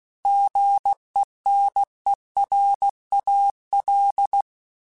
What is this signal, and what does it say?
Morse code, heard as a keyed tone, about 16 wpm, 800 Hz. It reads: GENERAL